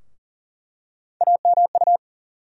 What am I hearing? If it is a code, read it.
Morse code, heard as a keyed tone, about 40 wpm, 700 Hz. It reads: AMU
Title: USA Call Sign Postfixes - 3 Letters 40wpm